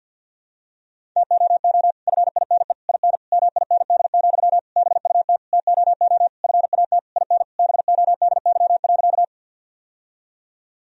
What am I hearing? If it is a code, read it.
Morse code, heard as a keyed tone, about 50 wpm, 700 Hz. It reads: TOO FINE IN MIND, BUT TOO FAT IN BODY.
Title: Sentences from Top 1000 Words - Review Part 4 - 50wpm